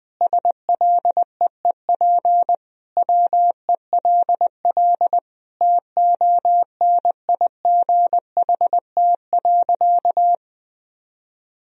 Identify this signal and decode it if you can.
Morse code, heard as a keyed tone, about 21 wpm, 700 Hz. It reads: SLEEP WELL TONIGHT.
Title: Sentences from Top 1000 Words - Review Part 4 - 20wpm